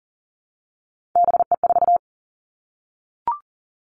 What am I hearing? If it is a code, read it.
Morse code, heard as a keyed tone, about 38 wpm, 700 Hz. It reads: BE4
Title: Three Challenging Letter-Number Combinations 40wpm